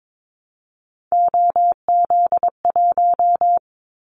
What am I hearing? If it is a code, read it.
Morse code, heard as a keyed tone, about 22 wpm, 700 Hz. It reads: OZ1
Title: Callsign Prefixes for Top 100 DXCC - 22wpm